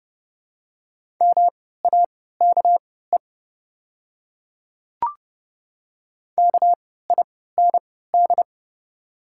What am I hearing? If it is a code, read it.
Morse code, heard as a keyed tone, about 31 wpm, spaced as for 17 wpm, 700 Hz. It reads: MAKE KIND
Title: 3x Character Spacing - Top 300 Words - 30wpm